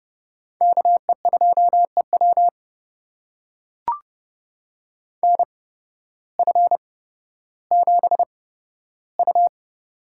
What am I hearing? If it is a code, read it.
Morse code, heard as a keyed tone, about 30 wpm, spaced as for 14 wpm, 700 Hz. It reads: KE2EW NF7U
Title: USA Call Signs - 8x Character Spacing - 30wpm